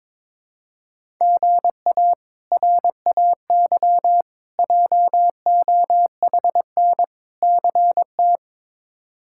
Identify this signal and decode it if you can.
Morse code, heard as a keyed tone, about 21 wpm, 700 Hz. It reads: GA RAY JOHN CT
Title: K1USN SST QSO Practice - 22wpm